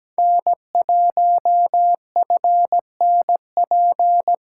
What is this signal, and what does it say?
Morse code, heard as a keyed tone, about 17 wpm, 700 Hz. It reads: N1FNP
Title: USA Callsign DE Callsign - 5 Characters - 17wpm